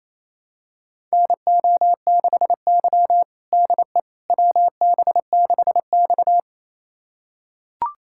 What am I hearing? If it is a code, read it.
Morse code, heard as a keyed tone, about 28 wpm, 700 Hz. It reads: NO6Y DE WB6X
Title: USA Callsign DE Callsign - 4 Characters - 28wpm